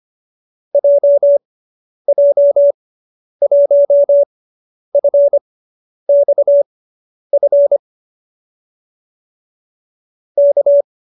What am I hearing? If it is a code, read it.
Morse code, heard as a keyed tone, about 25 wpm, spaced as for 10 wpm, 570 Hz. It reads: JJ1FXF K